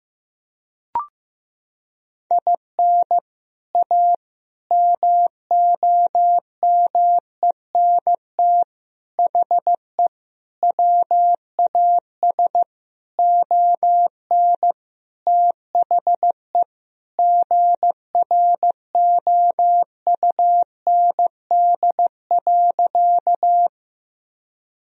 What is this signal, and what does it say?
Morse code, heard as a keyed tone, about 15 wpm, 700 Hz. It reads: IN A MOMENT HE WAS ON THE GROUND.
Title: Sentences from Top 1000 Words - Review Part 2 - 15wpm